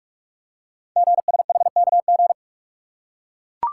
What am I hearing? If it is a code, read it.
Morse code, heard as a keyed tone, about 44 wpm, 700 Hz. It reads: GSHKG
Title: Code Groups - 5 Letters - 45wpm